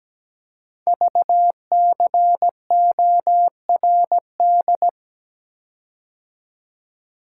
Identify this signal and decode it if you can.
Morse code, heard as a keyed tone, about 17 wpm, 700 Hz. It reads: VCORD
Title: Code Groups - 5 Letters - 17wpm